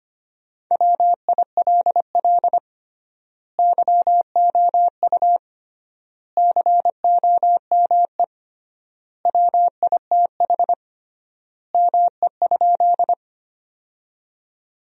Morse code, heard as a keyed tone, about 25 wpm, 700 Hz. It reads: WILL YOU COME WITH ME?
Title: 3x Word-Spacing - Sentences from Top 200 Words - 25wpm